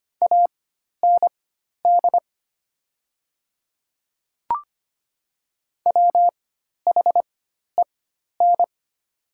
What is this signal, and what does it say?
Morse code, heard as a keyed tone, about 25 wpm, spaced as for 12 wpm, 700 Hz. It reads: AND WHEN
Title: 4x Character Spacing - Top 300 Words - 25wpm